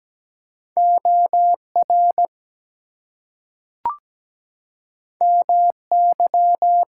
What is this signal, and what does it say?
Morse code, heard as a keyed tone, about 17 wpm, 700 Hz. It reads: OR MY
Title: Two Letter Words - 17wpm